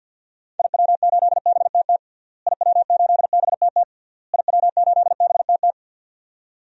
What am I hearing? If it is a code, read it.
Morse code, heard as a keyed tone, about 52 wpm, 700 Hz. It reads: IW8BTT IW8BTT IW8BTT
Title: DXCC-100 Call Signs - 6 Characters - Repeated 3x - 50wpm